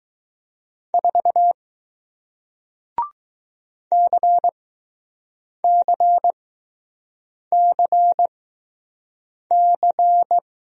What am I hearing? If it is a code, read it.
Morse code, heard as a keyed tone, about 20 wpm, spaced as for 6 wpm, 700 Hz. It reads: 4 CCCC